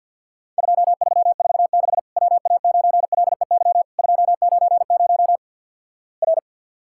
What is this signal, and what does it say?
Morse code, heard as a keyed tone, about 50 wpm, 700 Hz. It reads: 2346 WA9LEY 290 R
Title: Worked All Europe DX Contest - QTC Exchange - 50wpm